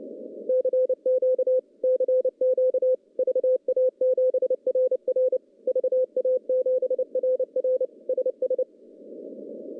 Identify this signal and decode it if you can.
Morse code, heard as a keyed tone, about 30 wpm, 500 Hz. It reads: CQ CQ VA7RR VA7RR SS
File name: contests / CW-SS-k5tr- -0900